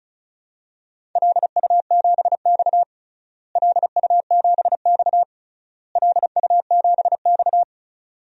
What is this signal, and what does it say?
Morse code, heard as a keyed tone, about 34 wpm, 700 Hz. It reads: LU7X LU7X LU7X
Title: DXCC-100 Call Signs - 4 Characters - Repeated 3x - 35wpm